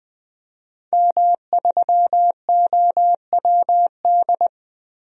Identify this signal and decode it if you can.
Morse code, heard as a keyed tone, about 20 wpm, 700 Hz. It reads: M3OWD
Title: DXCC-100 Call Signs - 5 Characters - 20wpm